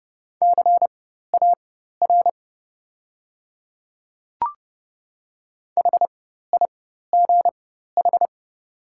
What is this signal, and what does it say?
Morse code, heard as a keyed tone, about 30 wpm, spaced as for 14 wpm, 700 Hz. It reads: CAR HIGH
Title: 4x Character Spacing - Top 300 Words - 30wpm